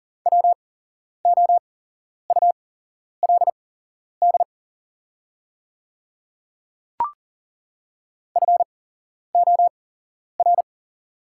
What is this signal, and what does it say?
Morse code, heard as a keyed tone, about 40 wpm, spaced as for 11 wpm, 700 Hz. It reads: WOULD FOR
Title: 8x Character Spacing - Top 100 Words - 40wpm